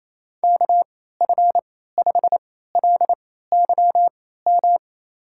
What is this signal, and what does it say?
Morse code, heard as a keyed tone, about 28 wpm, spaced as for 16 wpm, 700 Hz. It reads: KF5LYM
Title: USA Call Signs - 6 Characters - 3x Character Spacing - 28wpm